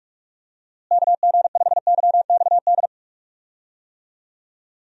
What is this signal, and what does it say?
Morse code, heard as a keyed tone, about 43 wpm, 700 Hz. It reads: KG5YXD
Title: USA Call Signs - 6 Characters - Repeated 3x - 45wpm